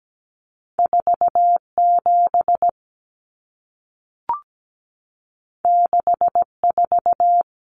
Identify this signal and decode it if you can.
Morse code, heard as a keyed tone, about 17 wpm, 700 Hz. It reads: 47 64
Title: Numbers - 2-Digit 17wpm